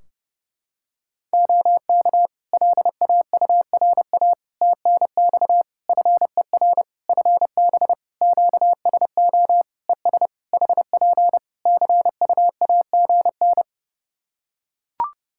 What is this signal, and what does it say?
Morse code, heard as a keyed tone, about 30 wpm, 700 Hz. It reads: OK LAURA TNX FER FB QSO ES HP CUAGN
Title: K1USN SST QSO Practice 30wpm